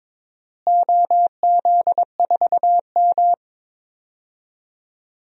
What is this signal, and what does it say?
Morse code, heard as a keyed tone, about 22 wpm, 700 Hz. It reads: OZ4M
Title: DXCC-100 Call Signs - 4 Characters - Repeated 2x - 22wpm